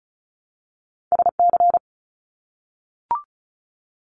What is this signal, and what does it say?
Morse code, heard as a keyed tone, about 36 wpm, 700 Hz. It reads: SC
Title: ARRL Sections 35wpm